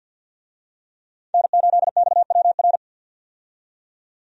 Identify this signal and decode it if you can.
Morse code, heard as a keyed tone, about 52 wpm, 700 Hz. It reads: N8XWR
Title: USA Call Signs - 5 Characters - 50wpm